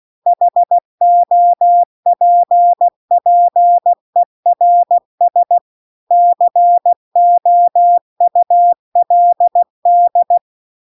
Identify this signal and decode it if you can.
Morse code, heard as a keyed tone, about 16 wpm, 700 Hz. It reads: HOPPERS COULD
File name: CW-The-Big-Twoheaded-River-2-16WPM